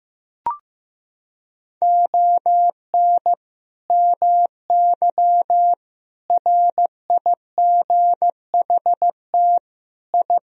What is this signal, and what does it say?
Morse code, heard as a keyed tone, about 15 wpm, 700 Hz. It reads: ON MY RIGHT I
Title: Sentences from Top 1000 Words - Review Part 2 - 15wpm